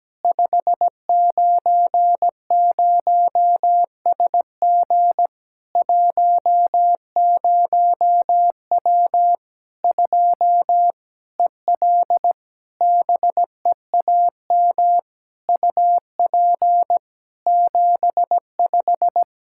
Morse code, heard as a keyed tone, about 17 wpm, 700 Hz. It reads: 590SG 10W 2 EL BEAM UP 75
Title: Practice QSO - Exchange 2 - Info - 17wpm